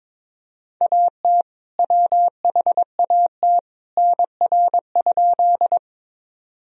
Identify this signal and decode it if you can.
Morse code, heard as a keyed tone, about 22 wpm, 700 Hz. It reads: AT WHAT NR?
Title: Sentences from Top 100 Words and Abbreviations 22wpm